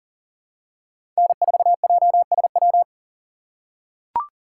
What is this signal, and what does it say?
Morse code, heard as a keyed tone, about 40 wpm, 700 Hz. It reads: N4JSW
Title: USA Call Signs - 5 Characters - 3x Character Spacing - 40wpm